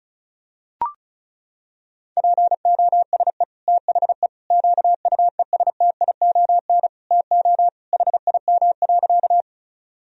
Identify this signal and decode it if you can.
Morse code, heard as a keyed tone, about 34 wpm, 700 Hz. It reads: POSE THE QUESTION TO HIM.
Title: Sentences from Top 1000 Words - Review Part 2 - 35wpm